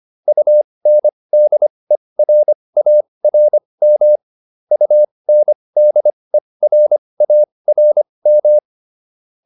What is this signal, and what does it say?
Morse code, heard as a keyed tone, about 25 wpm, spaced as for 20 wpm, 600 Hz. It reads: UNDERARM UNDERARM